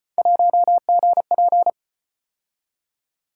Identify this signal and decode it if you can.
Morse code, heard as a keyed tone, about 38 wpm, 700 Hz. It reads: 1GP